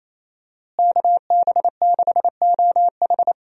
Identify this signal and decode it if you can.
Morse code, heard as a keyed tone, about 28 wpm, 700 Hz. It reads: KB6OH
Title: USA Call Signs - 5 Characters - Repeated 2x - 28wpm